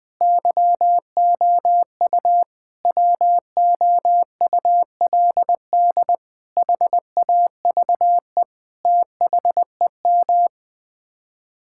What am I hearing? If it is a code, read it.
Morse code, heard as a keyed tone, about 20 wpm, 700 Hz. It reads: YOU WOULD HAVE THEM
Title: Sets of 8 Words - 20wpm